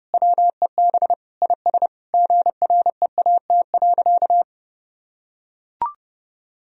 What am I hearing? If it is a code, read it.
Morse code, heard as a keyed tone, about 30 wpm, 700 Hz. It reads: WEB IS GREAT.
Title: Sentences from Top 1000 Words - Review Part 4 - 30wpm